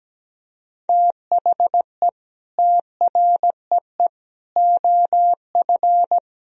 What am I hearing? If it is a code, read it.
Morse code, heard as a keyed tone, about 17 wpm, 700 Hz. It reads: THE TREE OF
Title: Sentences from Top 1000 Words - Review Part 1 - 17wpm